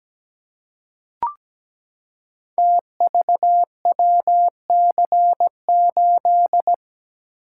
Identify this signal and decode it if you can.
Morse code, heard as a keyed tone, about 17 wpm, 700 Hz. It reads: TVWC8
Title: Code Groups - 5 Letter:Numbers - 17wpm